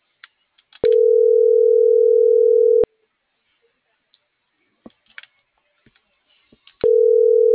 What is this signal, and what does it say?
Ringback tone (North America), 440 + 480 Hz.